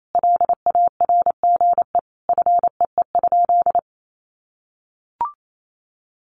Morse code, heard as a keyed tone, about 28 wpm, 700 Hz. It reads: LARGE FEE?
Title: Sentences from Top 1500 Words 28wpm